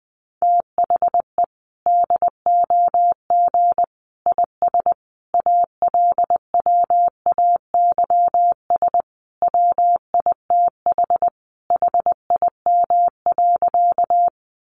Morse code, heard as a keyed tone, about 20 wpm, 700 Hz. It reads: THE DOG IS ALWAYS WITH HIM.
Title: Sentences from Top 400 Words 20wpm